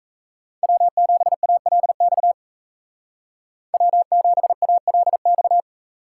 Morse code, heard as a keyed tone, about 33 wpm, 700 Hz. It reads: W7ALX W7ALX
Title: Speed Racing - USA Call Signs 28wpm